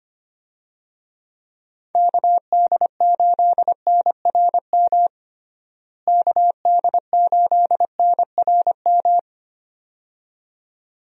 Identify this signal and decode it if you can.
Morse code, heard as a keyed tone, about 25 wpm, 700 Hz. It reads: KD8NRM KD8NRM
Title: USA Call Signs - 6 Characters - Repeated 2x - 25wpm